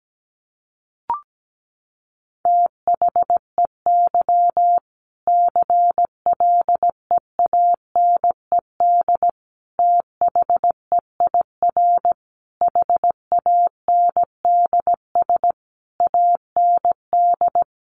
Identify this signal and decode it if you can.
Morse code, heard as a keyed tone, about 17 wpm, 700 Hz. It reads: THEY CLEANED THEIR HANDS AND
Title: A Sound Of Thunder 17wpm